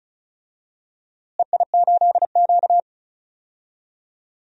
Morse code, heard as a keyed tone, about 36 wpm, 700 Hz. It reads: EI8Q
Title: DXCC-100 Call Signs - 4 Characters - Repeated 2x - 35wpm